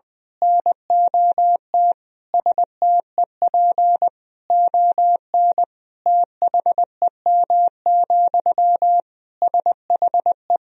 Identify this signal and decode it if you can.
Morse code, heard as a keyed tone, about 21 wpm, 700 Hz. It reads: NOT STEP ON THEM, SHE
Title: Sentences from Top 1000 Words - Review Part 2 - 20wpm